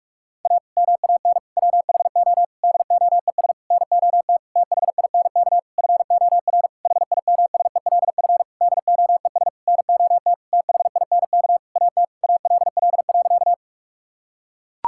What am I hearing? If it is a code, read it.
Morse code, heard as a keyed tone, about 42 wpm, 700 Hz. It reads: A MAN WHO DOES NOT THINK FOR HIMSELF DOES NOT THINK AT ALL.
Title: Sentences from Top 1600 Words 45wpm